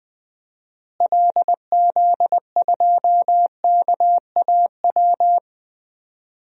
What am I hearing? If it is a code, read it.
Morse code, heard as a keyed tone, about 21 wpm, 700 Hz. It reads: LZ2KAW
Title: DXCC-100 Call Signs - 6 Characters - Repeated 2x - 20wpm